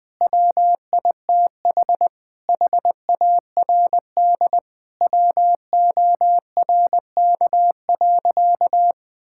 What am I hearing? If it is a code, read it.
Morse code, heard as a keyed tone, about 20 wpm, 700 Hz. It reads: WITH HARD WORK.
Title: Sentences with 6-Word Phrases - Spoken Beginning - 20wpm